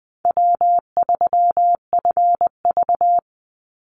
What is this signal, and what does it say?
Morse code, heard as a keyed tone, about 19 wpm, 700 Hz. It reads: W3FV
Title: Easy Call Signs Repeated 3x - 20wpm